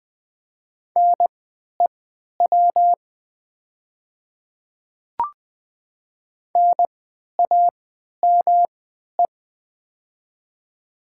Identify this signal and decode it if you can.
Morse code, heard as a keyed tone, about 20 wpm, spaced as for 11 wpm, 700 Hz. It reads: NEW NAME
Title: 3x Character Spacing - Top 200 Words - 20wpm